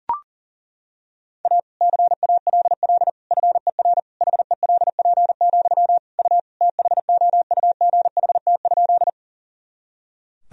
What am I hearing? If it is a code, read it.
Morse code, heard as a keyed tone, about 37 wpm, 700 Hz. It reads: A CALL FER HELP, U THOUGHT?
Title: Sentences from Top 300 Words and Abbreviations 40wpm